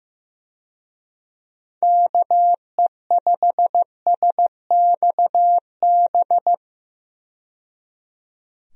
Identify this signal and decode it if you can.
Morse code, heard as a keyed tone, about 15 wpm, 700 Hz. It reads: KE5SXB
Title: USA Call Signs - 6 Characters - 15wpm